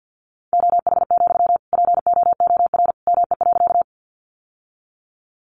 Morse code, heard as a keyed tone, about 50 wpm, 700 Hz. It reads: OH, POOR ME.
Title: Sentences from Top 1000 Words 50wpm